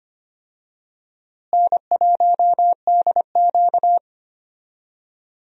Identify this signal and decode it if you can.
Morse code, heard as a keyed tone, about 25 wpm, 700 Hz. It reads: N1DQ
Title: USA Call Signs - 4 Characters - Repeated 2x - 25wpm